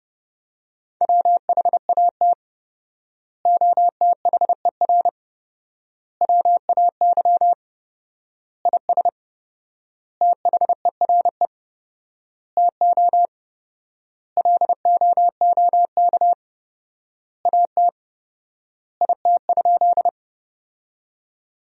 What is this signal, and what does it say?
Morse code, heard as a keyed tone, about 30 wpm, 700 Hz. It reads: WHAT OTHER WAY IS THERE TO LOOK AT IT?
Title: 4x Word-Spacing - Sentences from Top 100 Words - 30wpm